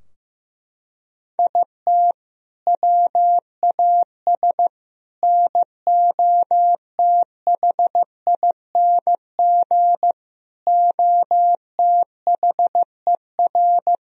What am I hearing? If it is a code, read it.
Morse code, heard as a keyed tone, about 16 wpm, 700 Hz. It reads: IT WAS NOTHING OTHER
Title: Sentences from Top 1000 Words - Review Part 2 - 15wpm